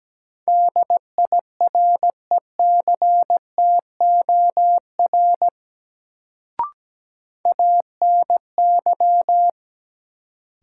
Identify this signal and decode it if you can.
Morse code, heard as a keyed tone, about 17 wpm, 700 Hz. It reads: DIRECTOR ANY
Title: Sets of 3 Words - New Word Drill - 17wpm